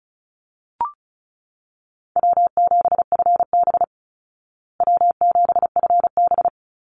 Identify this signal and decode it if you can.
Morse code, heard as a keyed tone, about 34 wpm, 700 Hz. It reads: W7FB W7FB
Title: Call Signs Repeated 3x - 35wpm